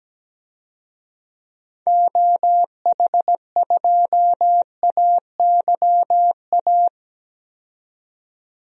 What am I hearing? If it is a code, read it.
Morse code, heard as a keyed tone, about 17 wpm, 700 Hz. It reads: OH2AYA
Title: DXCC-100 Call Signs - 6 Characters - 17wpm